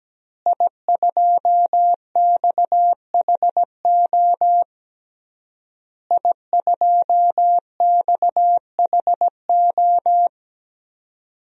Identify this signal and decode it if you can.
Morse code, heard as a keyed tone, about 17 wpm, 700 Hz. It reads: I2XHO I2XHO
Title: DXCC-100 Call Signs - 5 Characters - Repeated 3x - 17wpm